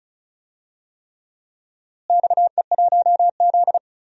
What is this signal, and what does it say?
Morse code, heard as a keyed tone, about 34 wpm, 700 Hz. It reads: XE1Z
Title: DXCC-100 Call Signs - 4 Characters - Repeated 3x - 35wpm